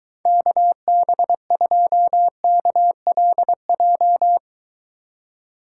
Morse code, heard as a keyed tone, about 23 wpm, 700 Hz. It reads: KB2KLJ